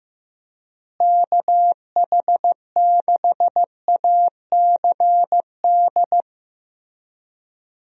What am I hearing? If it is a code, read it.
Morse code, heard as a keyed tone, about 15 wpm, 700 Hz. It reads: KH6ACD
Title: USA Call Signs - 6 Characters - Repeated 3x - 15wpm